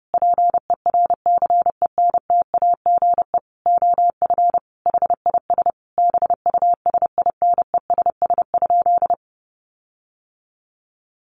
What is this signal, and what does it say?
Morse code, heard as a keyed tone, about 30 wpm, 700 Hz. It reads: PERCENTAGE OF HIS BUSINESS?
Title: Sentences from Top 1300 Words 30wpm